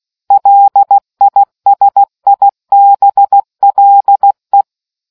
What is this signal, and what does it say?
Morse code, heard as a keyed tone, about 16 wpm, 800 Hz. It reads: LISIBLE